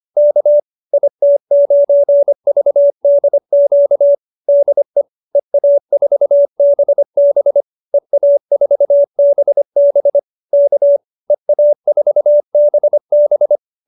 Morse code, heard as a keyed tone, about 25 wpm, 590 Hz. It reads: K IT9VDQ DE EA4BB EA4BB K EA4BB